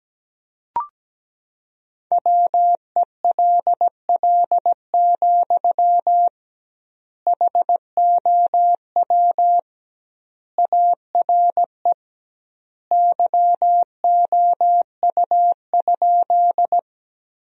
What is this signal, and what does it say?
Morse code, heard as a keyed tone, about 17 wpm, 700 Hz. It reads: WELL, HOW ARE YOU?
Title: 2x Word-Spacing - Sentences from Top 200 Words - 17wpm